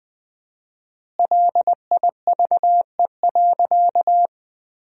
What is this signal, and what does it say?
Morse code, heard as a keyed tone, about 20 wpm, 700 Hz. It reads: LIVE.
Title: 4x Word-Spacing - Sentences from Top 200 Words - 20wpm